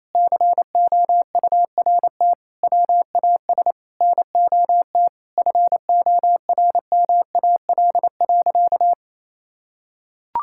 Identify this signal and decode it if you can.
Morse code, heard as a keyed tone, about 27 wpm, 700 Hz. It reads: COURT WAS NOT FORMAL.
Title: Sentences from Top 1600 Words 28wpm